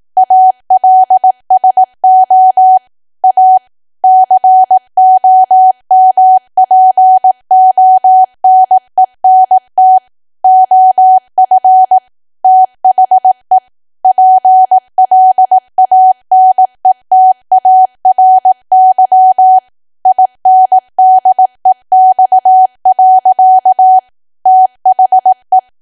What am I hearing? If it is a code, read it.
Morse code, heard as a keyed tone, about 18 wpm, 750 Hz. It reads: ALSO A COMPONENT OF THE PLANETARY INDEX. THE
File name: cwa / W1AW-Level2 / 18WPM